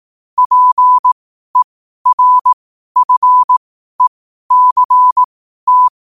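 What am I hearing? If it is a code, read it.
Morse code, heard as a keyed tone, about 18 wpm, spaced as for 12 wpm, 1000 Hz. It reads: PERFECT
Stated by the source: Morse code quote practice at 12 words per minute.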